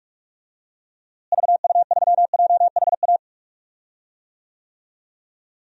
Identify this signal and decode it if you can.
Morse code, heard as a keyed tone, about 46 wpm, 700 Hz. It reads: VU3JHA
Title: DXCC-100 Call Signs - 6 Characters - Repeated 2x - 45wpm